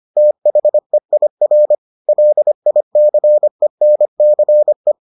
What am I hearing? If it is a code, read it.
Morse code, heard as a keyed tone, about 25 wpm, 600 Hz. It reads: THEIR LICENCE